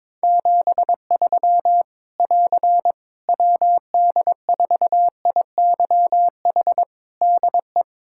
Morse code, heard as a keyed tone, about 22 wpm, 700 Hz. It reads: 73 <AR> WD4IYH DE
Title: Practice QSO - Exchange 4 - Ending - 22wpm